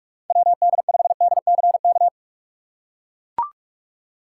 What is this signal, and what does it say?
Morse code, heard as a keyed tone, about 43 wpm, 700 Hz. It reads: WD5DCK
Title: USA Call Signs - 6 Characters - 3x Character Spacing - 45wpm